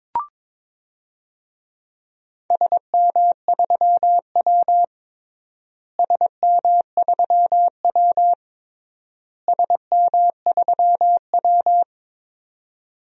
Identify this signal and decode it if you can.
Morse code, heard as a keyed tone, about 22 wpm, 700 Hz. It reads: SM3W SM3W SM3W
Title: DXCC-100 Call Signs - 4 Characters - Repeated 3x - 22wpm